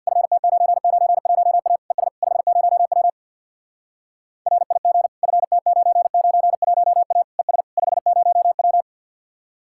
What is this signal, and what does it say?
Morse code, heard as a keyed tone, about 63 wpm, 700 Hz. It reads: FT991A ES 50W RIG FT991A ES 50W